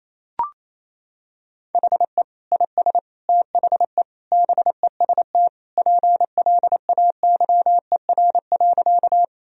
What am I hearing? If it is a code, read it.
Morse code, heard as a keyed tone, about 27 wpm, 700 Hz. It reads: HE IS THE BEST PLAYER.
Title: Sentences from Top 1000 Words - Review Part 4 - 28wpm